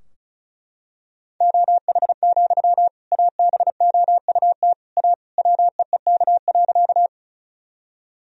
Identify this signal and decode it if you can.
Morse code, heard as a keyed tone, about 36 wpm, 700 Hz. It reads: OH, ABOUT A WEEK.
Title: Sentences from Top 1000 Words - Review Part 2 - 35wpm